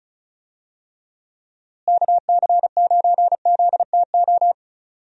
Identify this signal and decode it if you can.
Morse code, heard as a keyed tone, about 36 wpm, 700 Hz. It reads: KC9ZTO
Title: USA Call Signs - 6 Characters - Repeated 3x - 35wpm